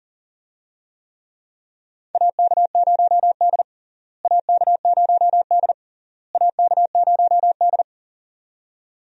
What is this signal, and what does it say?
Morse code, heard as a keyed tone, about 40 wpm, 700 Hz. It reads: AK0D AK0D AK0D
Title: USA Call Signs - 4 Characters - Repeated 3x - 40wpm